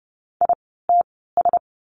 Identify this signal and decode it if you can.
Morse code, heard as a keyed tone, about 30 wpm, spaced as for 17 wpm, 700 Hz. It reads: ITS
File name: course / 30:10wpm / 115 Lesson 19f - Letter w - Words 30wpm